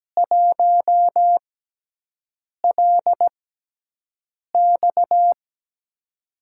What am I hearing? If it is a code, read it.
Morse code, heard as a keyed tone, about 17 wpm, spaced as for 6 wpm, 700 Hz. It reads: 1LX
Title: USA Call Signs - 5 Characters - 6x Character Spacing - 17wpm